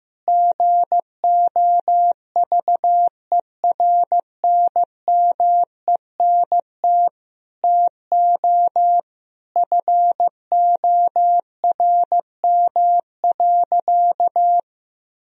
Morse code, heard as a keyed tone, about 15 wpm, 700 Hz. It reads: GOVERNMENT TO FORM.
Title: Sentences from Top 1000 Words - Review Part 1 - 15wpm